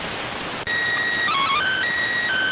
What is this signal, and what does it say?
SSTV (slow-scan television), PD 290.